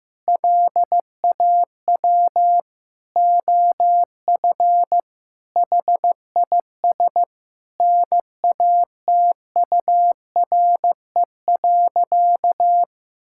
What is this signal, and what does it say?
Morse code, heard as a keyed tone, about 15 wpm, 700 Hz. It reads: LAW OF HIS NATURE.
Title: Sentences from Top 1000 Words - Review Part 5 - 15wpm